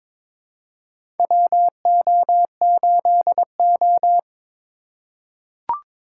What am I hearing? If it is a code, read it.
Morse code, heard as a keyed tone, about 22 wpm, 700 Hz. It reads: WO8O
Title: USA Call Signs - 4 Characters - 4x Character Spacing - 22wpm